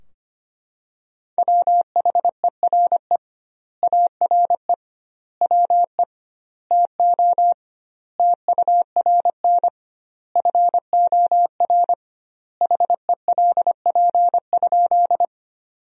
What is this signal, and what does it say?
Morse code, heard as a keyed tone, about 25 wpm, 700 Hz. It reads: WHERE ARE WE TO TURN FOR HELP?
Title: 2x Word-Spacing - Sentences from Top 200 Words - 25wpm